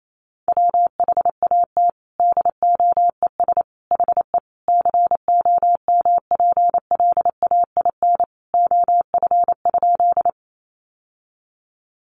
Morse code, heard as a keyed tone, about 28 wpm, 700 Hz. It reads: WHAT DOES HE COMPLAIN OF?
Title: Sentences from Top 1300 Words 28wpm